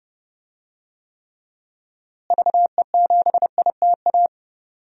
Morse code, heard as a keyed tone, about 31 wpm, 700 Hz. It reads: VE7ITA
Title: DXCC-100 Call Signs - 6 Characters - Repeated 2x - 30wpm